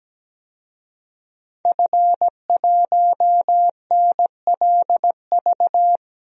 Morse code, heard as a keyed tone, about 17 wpm, 700 Hz. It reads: F1NLV
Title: DXCC-100 Call Signs - 5 Characters - Repeated 3x - 17wpm